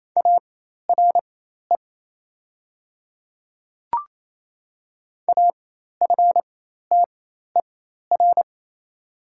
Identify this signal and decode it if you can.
Morse code, heard as a keyed tone, about 28 wpm, spaced as for 13 wpm, 690 Hz. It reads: ARE AFTER
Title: 4x Character Spacing - Top 200 Words - 28wpm